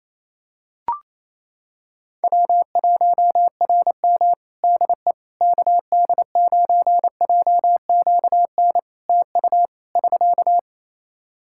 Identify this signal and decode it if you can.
Morse code, heard as a keyed tone, about 28 wpm, 700 Hz. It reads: W1RM DE KD9JQN TU <SK>
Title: K1USN SST QSO Practice 28wpm